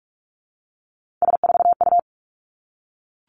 Morse code, heard as a keyed tone, about 46 wpm, 700 Hz. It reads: S4U
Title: Three Challenging Letter-Number Combinations 45wpm